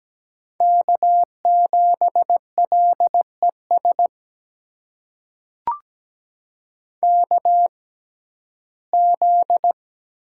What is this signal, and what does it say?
Morse code, heard as a keyed tone, about 17 wpm, spaced as for 12 wpm, 700 Hz. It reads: K7LES KZ
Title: USA Call Signs - 6x Character Spacing - 17wpm